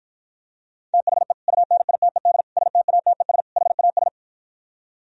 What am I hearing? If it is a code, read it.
Morse code, heard as a keyed tone, about 52 wpm, 700 Hz. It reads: THE UNITED STATES HAS